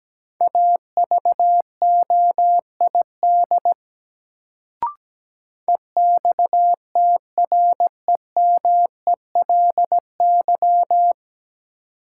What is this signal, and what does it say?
Morse code, heard as a keyed tone, about 17 wpm, 700 Hz. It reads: AVOID EXTREMELY
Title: Sentences from Top 1200 Words - New Word Drill Alt - 17wpm